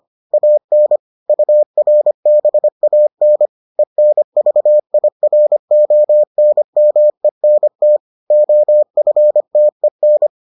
Morse code, heard as a keyed tone, about 25 wpm, 600 Hz. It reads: AN URBAN ENVIRONMENT OFTEN